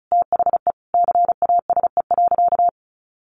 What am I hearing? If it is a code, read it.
Morse code, heard as a keyed tone, about 34 wpm, 700 Hz. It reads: THE CASE.
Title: Sentences from Top 1200 Words 35wpm